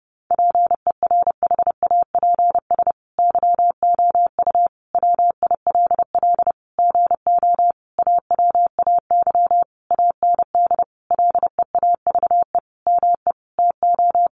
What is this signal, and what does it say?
Morse code, heard as a keyed tone, about 28 wpm, 700 Hz. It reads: PERHAPS YOU WILL GO AWAY AND LEAVE ME TO
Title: Sentences from Top 1300 Words 30wpm